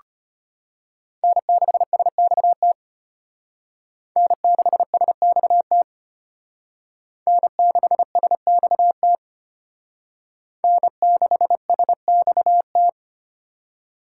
Morse code, heard as a keyed tone, about 32 wpm, 700 Hz. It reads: N6SXT N6SXT N6SXT N6SXT